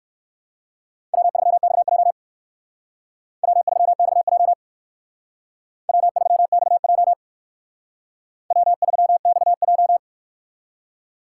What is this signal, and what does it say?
Morse code, heard as a keyed tone, about 56 wpm, 700 Hz. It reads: W3XJ W3XJ W3XJ W3XJ